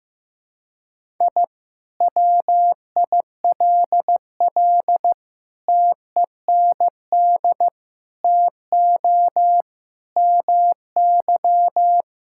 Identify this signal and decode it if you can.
Morse code, heard as a keyed tone, about 14 wpm, 700 Hz. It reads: I WILL TEND TO MY
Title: Sentences from Top 1000 Words - Review Part 5 - 15wpm